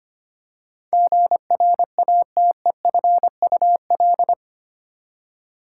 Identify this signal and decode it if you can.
Morse code, heard as a keyed tone, about 25 wpm, 700 Hz. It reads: GRATEFUL
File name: Eight Letter Words - 25wpm